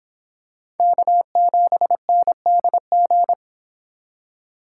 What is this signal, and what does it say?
Morse code, heard as a keyed tone, about 25 wpm, 700 Hz. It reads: K7NDG